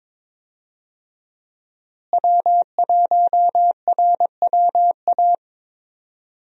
Morse code, heard as a keyed tone, about 21 wpm, 700 Hz. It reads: W1RWA
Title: USA Call Signs - 5 Characters - Repeated 2x - 22wpm